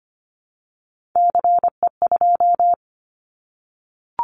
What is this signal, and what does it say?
Morse code, heard as a keyed tone, about 25 wpm, 700 Hz. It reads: CE2
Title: Callsign Prefixes for Top 100 DXCC - 25wpm